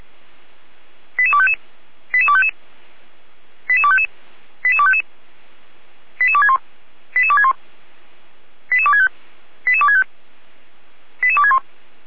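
Five-tone selective call (ZVEI), heard as quick runs of five tones, tones of about 70 ms. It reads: 88255 88255 88255 88255 88261 88261 88265 88265 88261
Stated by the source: Five-tone ZVEI selective call received on an SDR, from the audio.f85360.feuerwehr-orlamuende.org fire brigade server.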